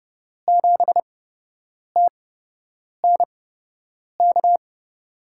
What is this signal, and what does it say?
Morse code, heard as a keyed tone, about 30 wpm, spaced as for 8 wpm, 700 Hz. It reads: 7TNK
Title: USA Call Signs - 5 Characters - 8x Character Spacing - 30wpm